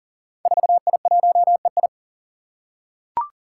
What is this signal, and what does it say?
Morse code, heard as a keyed tone, about 39 wpm, 700 Hz. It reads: 4I1EI
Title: Five Challenging Letter-Number Combinations 40wpm